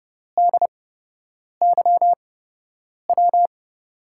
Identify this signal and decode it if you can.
Morse code, heard as a keyed tone, about 30 wpm, spaced as for 8 wpm, 700 Hz. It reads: DYW
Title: USA Call Signs - 8x Character Spacing - 30wpm